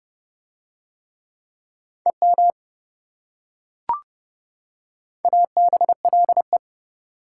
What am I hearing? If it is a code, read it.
Morse code, heard as a keyed tone, about 28 wpm, 700 Hz. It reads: EM ABLE